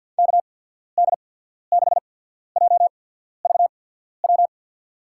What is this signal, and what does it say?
Morse code, heard as a keyed tone, about 53 wpm, 700 Hz. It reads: K D 6 J V W
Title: USA Call Signs - 6 Characters - 8x Character Spacing - 50wpm